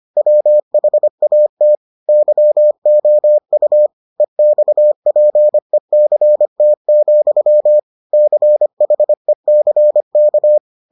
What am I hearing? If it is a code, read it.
Morse code, heard as a keyed tone, about 25 wpm, 600 Hz. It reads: WHAT YOU EXPECT, CHECK